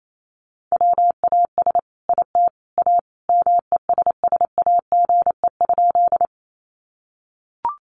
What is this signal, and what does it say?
Morse code, heard as a keyed tone, about 28 wpm, 700 Hz. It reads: WAS IT A MESSAGE?
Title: Sentences from Top 1100 Words 28wpm